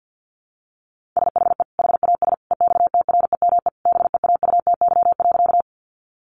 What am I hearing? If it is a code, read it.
Morse code, heard as a keyed tone, about 50 wpm, 700 Hz. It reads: SHE HAS EXTREME BEAUTY.
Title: Sentences from Top 1300 Words 50wpm